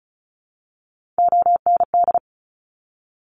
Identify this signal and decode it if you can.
Morse code, heard as a keyed tone, about 36 wpm, 700 Hz. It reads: OND
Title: Top Three Letter Combinations in Words 35wpm